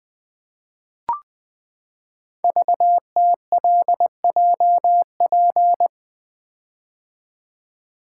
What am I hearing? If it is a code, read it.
Morse code, heard as a keyed tone, about 20 wpm, 700 Hz. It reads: VTLJP
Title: Code Groups - 5 Letters - 20wpm